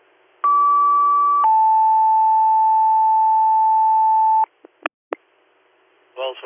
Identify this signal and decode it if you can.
Two-tone page, heard as a one-second tone, then a three-second tone, about 1150 then 880 Hz, with speech. It is heard from 0.4 to 4.4 s.